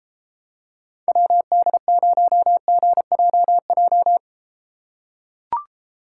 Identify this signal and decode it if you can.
Morse code, heard as a keyed tone, about 34 wpm, 700 Hz. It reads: WD0GJJ